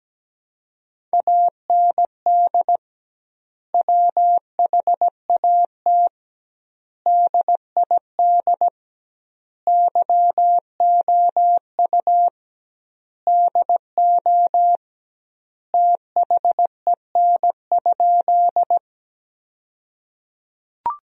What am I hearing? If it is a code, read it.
Morse code, heard as a keyed tone, about 17 wpm, 700 Hz. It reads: AND WHAT DID YOU DO THEN?
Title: 2x Word-Spacing - Sentences from Top 100 Words - 17wpm